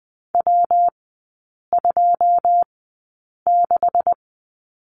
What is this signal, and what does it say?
Morse code, heard as a keyed tone, about 20 wpm, spaced as for 8 wpm, 700 Hz. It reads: W26
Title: 138 Lesson 23e - Letter f - Three Characters 20wpm